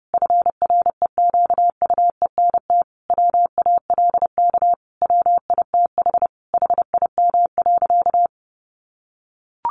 Morse code, heard as a keyed tone, about 30 wpm, 700 Hz. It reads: FREQUENT WALK WITH HIM.
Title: Sentences from Top 1400 Words 30wpm